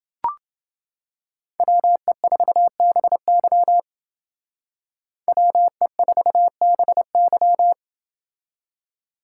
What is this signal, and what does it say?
Morse code, heard as a keyed tone, about 28 wpm, 700 Hz. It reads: WE4BY WE4BY